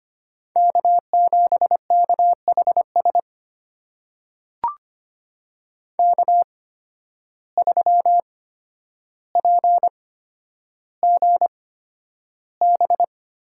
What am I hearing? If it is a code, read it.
Morse code, heard as a keyed tone, about 25 wpm, spaced as for 11 wpm, 700 Hz. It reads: K7KHS K3PGB
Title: USA Call Signs - 5 Characters - 8x Character Spacing - 25wpm